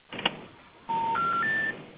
Special information tone, with noise.